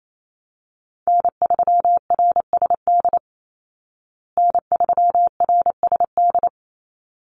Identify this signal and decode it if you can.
Morse code, heard as a keyed tone, about 28 wpm, 700 Hz. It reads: N3RSD N3RSD
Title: Call Signs Repeated 3x - 28wpm